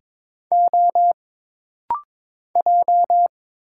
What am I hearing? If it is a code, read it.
Morse code, heard as a keyed tone, about 22 wpm, 700 Hz. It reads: O J
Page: This rapid-fire timing is geared to push your ICR (Instant Character Recognition) skills.